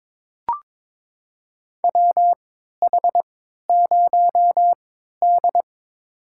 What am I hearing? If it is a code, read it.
Morse code, heard as a keyed tone, about 22 wpm, spaced as for 13 wpm, 700 Hz. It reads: WH0D
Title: USA Call Signs - 4 Characters - 3x Character Spacing - 22wpm